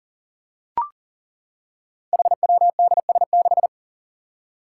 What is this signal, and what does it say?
Morse code, heard as a keyed tone, about 39 wpm, 700 Hz. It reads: HWDS6
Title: Five Challenging Letter-Number Combinations 40wpm